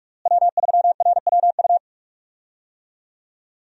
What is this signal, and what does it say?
Morse code, heard as a keyed tone, about 41 wpm, 700 Hz. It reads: W3RWU
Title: USA Call Signs - 5 Characters - Repeated 3x - 45wpm